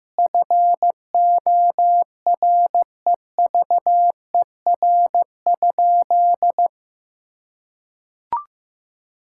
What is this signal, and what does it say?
Morse code, heard as a keyed tone, about 16 wpm, 700 Hz. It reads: FOREVER?
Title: Sentences from Top 1600 Words 15wpm